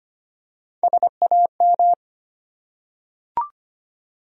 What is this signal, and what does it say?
Morse code, heard as a keyed tone, about 25 wpm, 700 Hz. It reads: SAM
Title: USA Common Names - Max 4 Letters - 25wpm